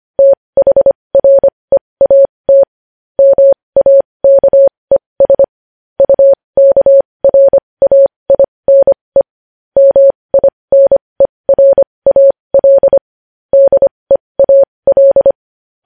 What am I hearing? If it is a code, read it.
Morse code, heard as a keyed tone, about 25 wpm, spaced as for 20 wpm, 550 Hz. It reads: THREAT MAKES UKRAINE MINERAL DEAL